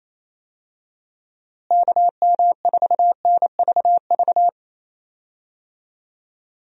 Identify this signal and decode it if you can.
Morse code, heard as a keyed tone, about 27 wpm, 700 Hz. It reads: KM4NVV
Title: USA Call Signs - 6 Characters - Repeated 2x - 28wpm